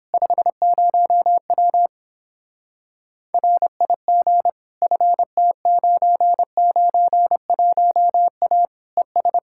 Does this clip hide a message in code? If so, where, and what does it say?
Yes: Morse code from 0.1 to 9.4 s, heard as a keyed tone, about 29 wpm, 700 Hz. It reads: 50W RIG FT991A ES